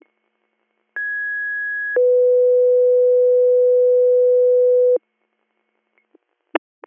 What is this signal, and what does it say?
Two-tone page, heard as a one-second tone, then a three-second tone, about 1650 then 500 Hz. It is heard from 1.0 to 5.0 s.